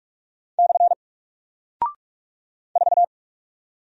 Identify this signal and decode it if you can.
Morse code, heard as a keyed tone, about 45 wpm, 700 Hz. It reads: / 4
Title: Single Character (Letter:Number:Punctuation:Prosign) Alt 45wpm